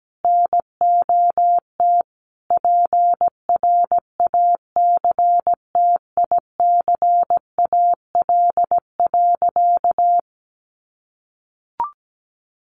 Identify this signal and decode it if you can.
Morse code, heard as a keyed tone, about 17 wpm, 700 Hz. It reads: NOT PRACTICAL.
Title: Sentences from Top 1400 Words 17wpm